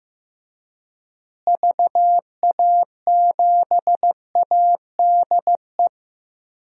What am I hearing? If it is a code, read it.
Morse code, heard as a keyed tone, about 16 wpm, 700 Hz. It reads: VA7ADE
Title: DXCC-100 Call Signs - 6 Characters - Repeated 2x - 15wpm